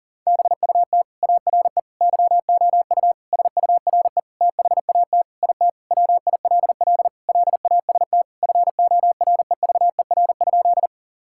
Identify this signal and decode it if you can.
Morse code, heard as a keyed tone, about 40 wpm, 700 Hz. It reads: BUT ARE YOU SURE THAT IT WILL LAST FOREVER?
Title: Sentences from Top 1600 Words 40wpm